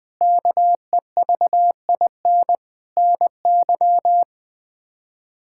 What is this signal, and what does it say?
Morse code, heard as a keyed tone, about 20 wpm, 700 Hz. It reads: KEVIN NY
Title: K1USN SST QSO Practice - 20wpm